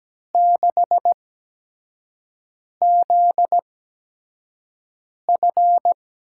Morse code, heard as a keyed tone, about 17 wpm, spaced as for 5 wpm, 700 Hz. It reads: 6ZF
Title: USA Call Signs - 6 Characters - 8x Character Spacing - 17wpm